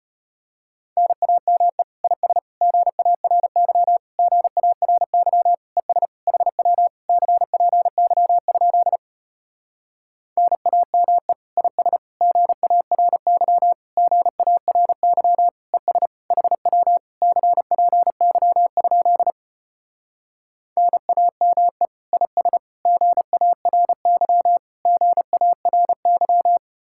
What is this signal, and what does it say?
Morse code, heard as a keyed tone, about 33 wpm, 700 Hz. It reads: NAME IS GARY GARY ES HW CPY? NAME IS GARY GARY ES HW CPY? NAME IS GARY GARY